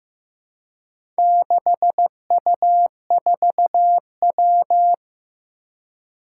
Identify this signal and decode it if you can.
Morse code, heard as a keyed tone, about 15 wpm, 700 Hz. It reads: 6U4W
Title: Four Challenging Letter-Number Combinations 15wpm